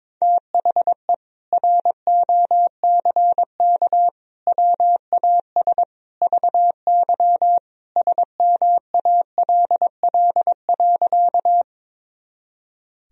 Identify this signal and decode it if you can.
Morse code, heard as a keyed tone, about 22 wpm, 700 Hz. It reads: THE ROCK WAS VY SMALL.
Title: Sentences from Top 400 Words and Abbreviations 22wpm